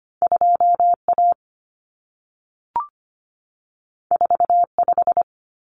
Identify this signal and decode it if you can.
Morse code, heard as a keyed tone, about 25 wpm, 700 Hz. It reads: 2A 45
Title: Common Cut Numbers - 2-Digit 25wpm